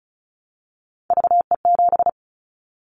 Morse code, heard as a keyed tone, about 34 wpm, 700 Hz. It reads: VE7
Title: Callsign Prefixes for Top 50 DXCC - 35wpm